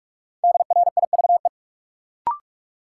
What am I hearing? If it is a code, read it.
Morse code, heard as a keyed tone, about 41 wpm, 700 Hz. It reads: DRIVE
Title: Sentences from Top 500 Words - New Word Drill - 45wpm